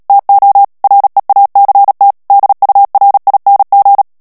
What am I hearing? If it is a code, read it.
Morse code, heard as a keyed tone, about 37 wpm, 800 Hz. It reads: TO REACT DURING
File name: morse mp3 / ANGLAIS texte clair 2 vitesse 35 mots minute